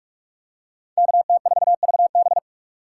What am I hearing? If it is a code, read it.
Morse code, heard as a keyed tone, about 46 wpm, 700 Hz. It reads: KT4VB
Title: USA Call Signs - 5 Characters - 4x Character Spacing - 45wpm